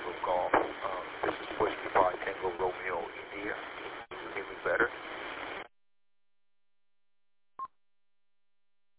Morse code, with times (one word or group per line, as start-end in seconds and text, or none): none